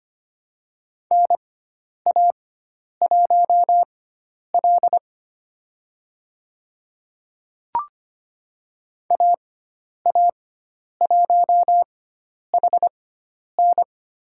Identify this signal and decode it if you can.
Morse code, heard as a keyed tone, about 25 wpm, spaced as for 10 wpm, 700 Hz. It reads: NA1L AA1HN
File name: Lesson 16g - Letter c - Call Signs 25wpm